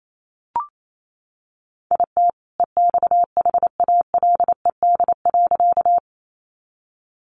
Morse code, heard as a keyed tone, about 28 wpm, 700 Hz. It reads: IT EXHALED.
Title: A Sound Of Thunder 28wpm